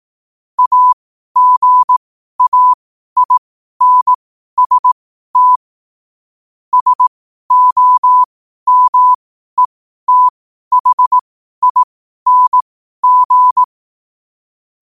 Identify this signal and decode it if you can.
Morse code, heard as a keyed tone, about 18 wpm, spaced as for 12 wpm, 1000 Hz. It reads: AGAINST SOMETHING
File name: qotdfiles / 12wpm